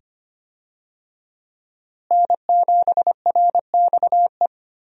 Morse code, heard as a keyed tone, about 25 wpm, 700 Hz. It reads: N7RXE
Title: USA Call Signs - 5 Characters - Repeated 2x - 25wpm